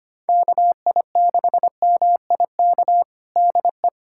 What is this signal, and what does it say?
Morse code, heard as a keyed tone, about 25 wpm, 700 Hz. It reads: KI6MIK DE
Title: USA Callsign DE Callsign - 6 Characters - 25wpm